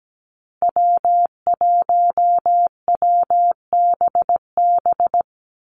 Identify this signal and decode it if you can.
Morse code, heard as a keyed tone, about 17 wpm, 700 Hz. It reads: W1WBB
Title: Call Signs Repeated 2x - 17wpm